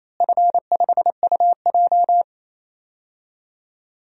Morse code, heard as a keyed tone, about 28 wpm, 700 Hz. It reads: F5UJ
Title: DXCC-100 Call Signs - 4 Characters - Repeated 2x - 28wpm